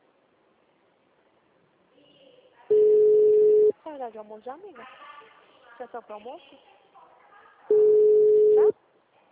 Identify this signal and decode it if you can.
Ringback tone (Europe, 425 Hz), with speech.
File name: out- -2014- - - .67476